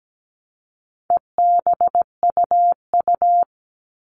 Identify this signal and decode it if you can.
Morse code, heard as a keyed tone, about 17 wpm, 700 Hz. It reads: EBUU
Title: All Cut Numbers - 4-Digit 17wpm